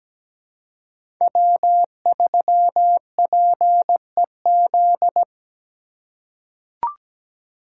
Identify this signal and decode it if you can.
Morse code, heard as a keyed tone, about 17 wpm, 700 Hz. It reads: W3PEZ
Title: USA Call Signs - 5 Characters - 4x Character Spacing - 17wpm